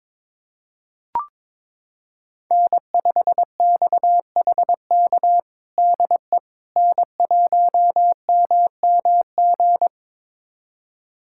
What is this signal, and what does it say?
Morse code, heard as a keyed tone, about 21 wpm, 700 Hz. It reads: N5XHK DE N1MMG
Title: USA Callsign DE Callsign - 5 Characters - 22wpm